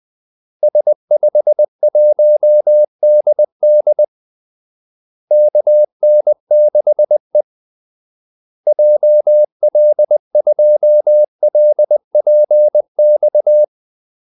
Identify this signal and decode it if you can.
Morse code, heard as a keyed tone, about 20 wpm, 600 Hz. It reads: S51DD KN6E JL2LPX
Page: Kaldesignaler 20 wpm